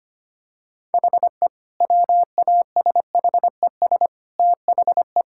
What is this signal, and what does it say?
Morse code, heard as a keyed tone, about 25 wpm, 700 Hz. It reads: HE WASHES THE
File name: practice / 25wpm / Difficult Sentences with Hints - 25wpm